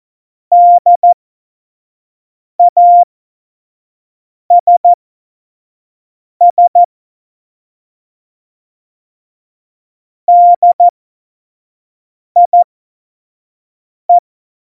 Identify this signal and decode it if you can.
Morse code, heard as a keyed tone, about 14 wpm, spaced as for 5 wpm, 700 Hz. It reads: DASS DIE
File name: Morsekurs / Witze / Witz 09. Januar 2018, 25 F70